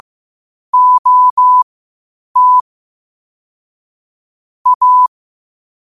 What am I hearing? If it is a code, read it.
Morse code, heard as a keyed tone, about 15 wpm, spaced as for 8 wpm, 1000 Hz. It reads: OT A
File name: qotdfiles / 8wpm